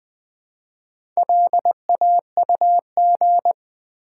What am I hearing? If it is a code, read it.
Morse code, heard as a keyed tone, about 20 wpm, 700 Hz. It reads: LAUG
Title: Build A Word - 1 Letter At A Time - 5-Letter Words - 20wpm